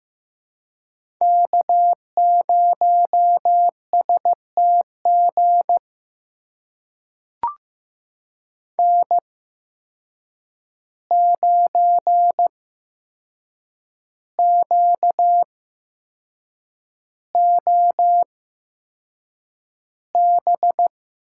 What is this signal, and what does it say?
Morse code, heard as a keyed tone, about 15 wpm, spaced as for 6 wpm, 700 Hz. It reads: K0STG N9QOB
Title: USA Call Signs - 5 Characters - 8x Character Spacing - 15wpm